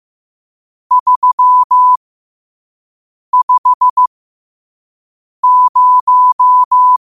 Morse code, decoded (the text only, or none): 350